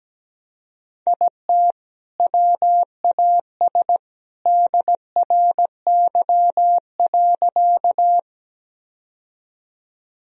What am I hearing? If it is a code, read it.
Morse code, heard as a keyed tone, about 17 wpm, 700 Hz. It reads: IT WAS DRY.
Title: Sentences from Top 1000 Words - Review Part 3 - 17wpm